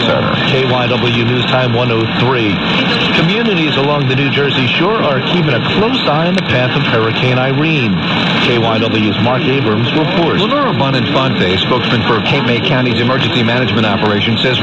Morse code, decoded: none

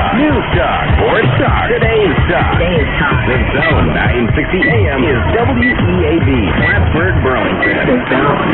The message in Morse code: none

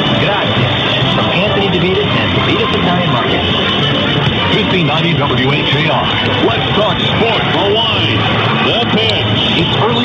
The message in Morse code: none

